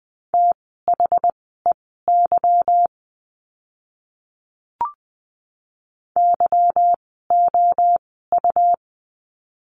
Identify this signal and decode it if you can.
Morse code, heard as a keyed tone, about 20 wpm, spaced as for 14 wpm, 700 Hz. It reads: THEY YOU
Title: 192 Lesson 31f - Letter v - Words 20wpm